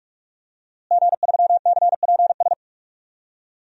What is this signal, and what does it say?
Morse code, heard as a keyed tone, about 46 wpm, 700 Hz. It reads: G3CPS
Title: DXCC-100 Call Signs - 5 Characters - Repeated 2x - 45wpm